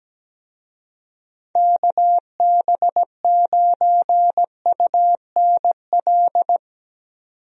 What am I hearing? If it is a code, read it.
Morse code, heard as a keyed tone, about 17 wpm, 700 Hz. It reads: KB9UNL
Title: USA Call Signs - 6 Characters - 17wpm